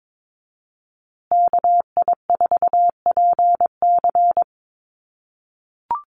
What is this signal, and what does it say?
Morse code, heard as a keyed tone, about 22 wpm, 700 Hz. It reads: KI4PC
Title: USA Call Signs Repeated 3x 22wpm